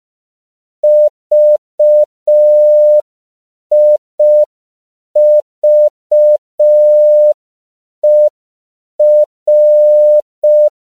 Morse code, heard as a keyed tone, about 5 wpm, 600 Hz. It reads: VIVER
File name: CW / 5PPM